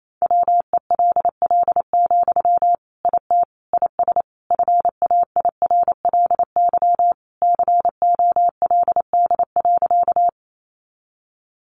Morse code, heard as a keyed tone, about 28 wpm, 700 Hz. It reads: WELL, IT IS FAIRLY COLD.
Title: Sentences from Top 1100 Words 28wpm